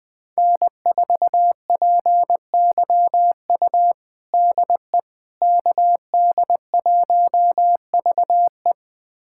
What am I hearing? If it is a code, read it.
Morse code, heard as a keyed tone, about 21 wpm, 700 Hz. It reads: N4PYU DE KD1VE
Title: USA Callsign DE Callsign - 5 Characters - 20wpm